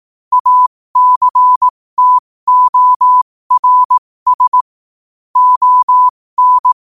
Morse code, heard as a keyed tone, about 18 wpm, spaced as for 15 wpm, 1000 Hz. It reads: ACTORS ON
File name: qotdfiles / 15wpm